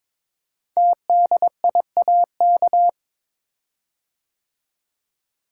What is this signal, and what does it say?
Morse code, heard as a keyed tone, about 23 wpm, 700 Hz. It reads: TDIAK
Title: Code Groups - 5 Letters - 22wpm